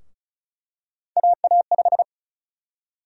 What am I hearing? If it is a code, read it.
Morse code, heard as a keyed tone, about 36 wpm, 700 Hz. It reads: AA5
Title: USA Call Sign Prefixes - Repeated 2x - 35wpm